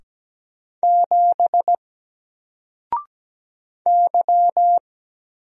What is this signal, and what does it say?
Morse code, heard as a keyed tone, about 17 wpm, 700 Hz. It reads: 7 Y
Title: Single Character (Letter:Number:Punctuation:Prosign) 17wpm